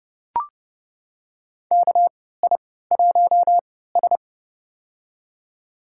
Morse code, heard as a keyed tone, about 32 wpm, spaced as for 17 wpm, 700 Hz. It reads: KI1S
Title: USA Call Signs - 3x Character Spacing - 30wpm